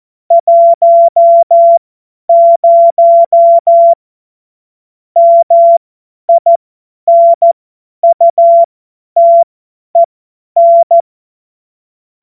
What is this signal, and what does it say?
Morse code, heard as a keyed tone, about 14 wpm, spaced as for 10 wpm, 680 Hz. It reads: 10 MINUTEN
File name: Morsekurs / Witze / Witz 06. September 2022 50 F70